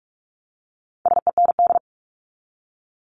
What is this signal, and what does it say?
Morse code, heard as a keyed tone, about 44 wpm, 700 Hz. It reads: SEND
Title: Top 1251-1500 Words 45wpm